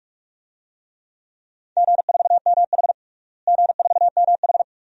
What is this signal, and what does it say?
Morse code, heard as a keyed tone, about 41 wpm, 700 Hz. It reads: G4MH G4MH
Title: DXCC-100 Call Signs - 4 Characters - Repeated 2x - 45wpm